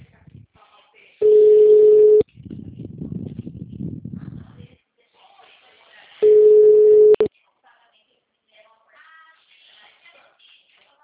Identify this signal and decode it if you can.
Ringback tone (Europe, 425 Hz), with speech.
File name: out- -2012- - - .582